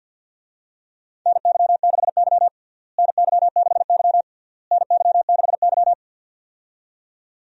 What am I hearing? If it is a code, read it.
Morse code, heard as a keyed tone, about 52 wpm, 700 Hz. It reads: NY6Y NY6Y NY6Y
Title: USA Call Signs - 4 Characters - Repeated 3x - 50wpm